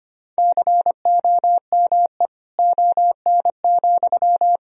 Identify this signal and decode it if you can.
Morse code, heard as a keyed tone, about 25 wpm, 700 Hz. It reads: COME ON,
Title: Sentences from Top 1000 Words - Review Part 1 - 25wpm